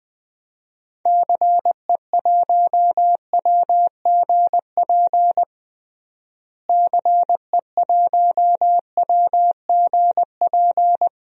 Morse code, heard as a keyed tone, about 20 wpm, 700 Hz. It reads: CE1WGP CE1WGP
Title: DXCC-100 Call Signs - 6 Characters - Repeated 3x - 20wpm